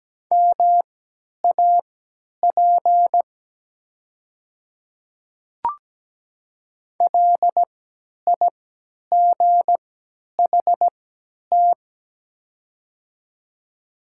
Morse code, heard as a keyed tone, about 17 wpm, spaced as for 10 wpm, 700 Hz. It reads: MAP LIGHT
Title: 3x Character Spacing - Top 500 Words - 17wpm